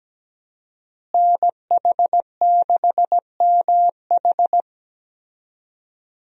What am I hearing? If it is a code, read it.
Morse code, heard as a keyed tone, about 17 wpm, 700 Hz. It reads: NH6MH
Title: USA Call Signs - 5 Characters - Repeated 2x - 17wpm